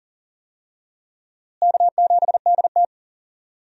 Morse code, heard as a keyed tone, about 37 wpm, 700 Hz. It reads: K7DT
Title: USA Call Signs - 4 Characters - Repeated 2x - 40wpm